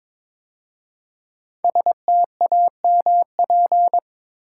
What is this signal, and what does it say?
Morse code, heard as a keyed tone, about 22 wpm, 700 Hz. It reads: STAMP
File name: 5 Letter Words - First 3 Letters Given - 22wpm